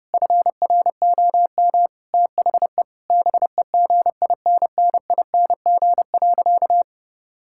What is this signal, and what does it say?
Morse code, heard as a keyed tone, about 30 wpm, 700 Hz. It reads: FROM THE BEGINNING.
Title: Sentences from Top 1000 Words - Review Part 5 - 30wpm